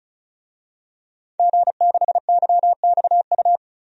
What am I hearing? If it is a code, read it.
Morse code, heard as a keyed tone, about 34 wpm, 700 Hz. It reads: G6YXU
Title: DXCC-100 Call Signs - 5 Characters - 35wpm